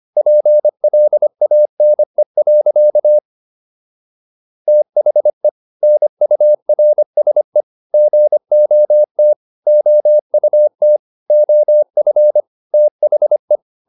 Morse code, heard as a keyed tone, about 25 wpm, 600 Hz. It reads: PLANE. THE NURSE GOT OUT OF THE